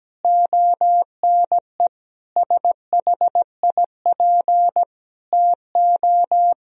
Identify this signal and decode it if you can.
Morse code, heard as a keyed tone, about 17 wpm, 700 Hz. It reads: ONE SHIP TO
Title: Sentences from Top 1000 Words - Review Part 2 - 17wpm